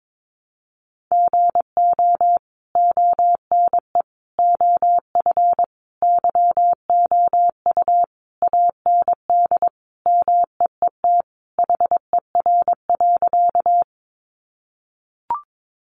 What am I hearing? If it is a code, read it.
Morse code, heard as a keyed tone, about 22 wpm, 700 Hz. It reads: GO ONE OF YOU AND MEET HER.
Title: Sentences from Top 1100 Words 22wpm